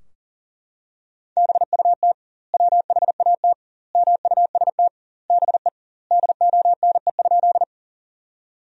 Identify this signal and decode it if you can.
Morse code, heard as a keyed tone, about 40 wpm, 700 Hz. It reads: BUT WHAT MUST BE DONE?
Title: 2x Word-Spacing - Sentences from Top 500 Words - 40wpm